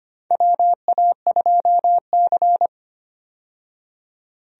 Morse code, heard as a keyed tone, about 25 wpm, 700 Hz. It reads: WA2C
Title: USA Call Signs - 4 Characters - 25wpm